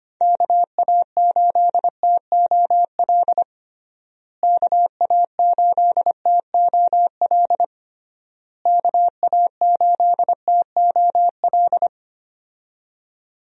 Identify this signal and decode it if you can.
Morse code, heard as a keyed tone, about 25 wpm, 700 Hz. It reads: KA8TOL KA8TOL KA8TOL
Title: USA Call Signs - 6 Characters - Repeated 3x - 25wpm